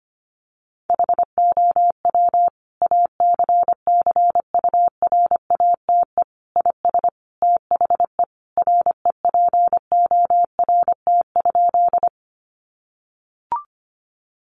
Morse code, heard as a keyed tone, about 25 wpm, 700 Hz. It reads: HOW ACCURATE IS THE REPORT?
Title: Sentences from Top 1200 Words 25wpm